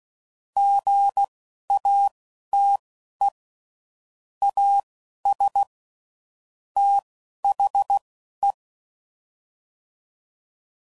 Morse code, heard as a keyed tone, about 16 wpm, spaced as for 11 wpm, 800 Hz. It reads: GATE AS THE
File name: CW Code Groups @10WPM / Technical Subjects / OR Gate